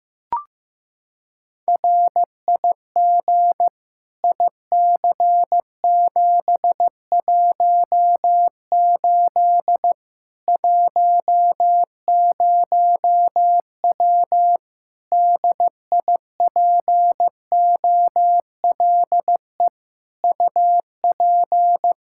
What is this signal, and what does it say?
Morse code, heard as a keyed tone, about 15 wpm, 700 Hz. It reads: RIG IC718 10W DIPOLE UP
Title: Practice QSO - Exchange 2 - Info - 15wpm